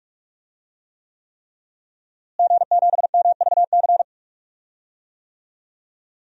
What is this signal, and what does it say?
Morse code, heard as a keyed tone, about 45 wpm, 700 Hz. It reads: G7MVC
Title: DXCC-100 Call Signs - 5 Characters - Repeated 3x - 45wpm